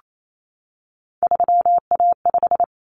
Morse code, heard as a keyed tone, about 28 wpm, 700 Hz. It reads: 3A5
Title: Common Cut Numbers - 3-Digit 28wpm